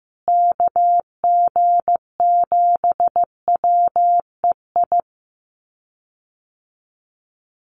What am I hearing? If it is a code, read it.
Morse code, heard as a keyed tone, about 15 wpm, 700 Hz. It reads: KG7WEI
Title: USA Call Signs Repeated 2x 15wpm